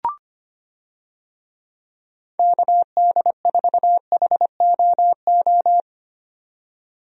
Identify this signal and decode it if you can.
Morse code, heard as a keyed tone, about 25 wpm, 700 Hz. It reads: KD4HOO
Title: USA Call Signs - 6 Characters - 25wpm